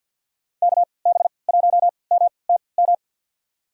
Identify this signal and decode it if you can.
Morse code, heard as a keyed tone, about 52 wpm, spaced as for 29 wpm, 700 Hz. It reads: KB1MTM
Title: USA Call Signs - 6 Characters - 3x Character Spacing - 50wpm